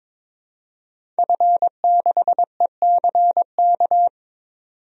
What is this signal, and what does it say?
Morse code, heard as a keyed tone, about 21 wpm, 700 Hz. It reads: F6ECK
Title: DXCC-100 Call Signs - 5 Characters - 22wpm